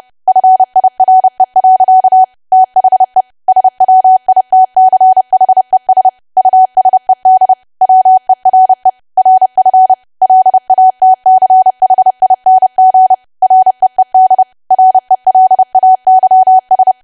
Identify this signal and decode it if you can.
Morse code, heard as a keyed tone, about 30 wpm, 750 Hz. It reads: FIRE. THE SWITCHES USED WERE RF LATCHING REED RELAYS
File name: cwa / W1AW-Level3 / 30WPM